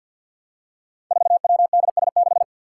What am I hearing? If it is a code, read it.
Morse code, heard as a keyed tone, about 45 wpm, 700 Hz. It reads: 4WDS6
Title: Five Challenging Letter-Number Combinations 50wpm